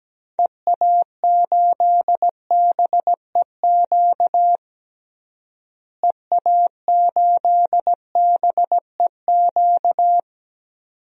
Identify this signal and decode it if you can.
Morse code, heard as a keyed tone, about 17 wpm, 700 Hz. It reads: EA8BEQ EA8BEQ
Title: DXCC-100 Call Signs - 6 Characters - Repeated 2x - 17wpm